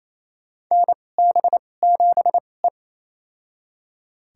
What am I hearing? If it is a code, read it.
Morse code, heard as a keyed tone, about 28 wpm, spaced as for 20 wpm, 700 Hz. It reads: NB7E
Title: USA Call Signs - 4 Characters - 2x Character Spacing - 28wpm